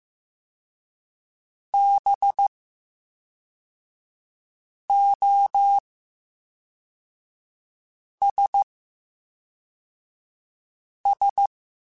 Morse code, heard as a keyed tone, about 15 wpm, 800 Hz. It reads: B O S S
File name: Special INGs / Special INGs-mp3s / VerySlowToFaster Ing @10